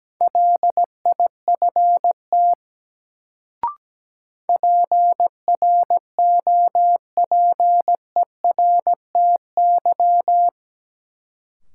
Morse code, heard as a keyed tone, about 17 wpm, 700 Hz. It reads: LIFT PROPERTY
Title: Sentences from Top 1000 Words - New Word Drill - 17wpm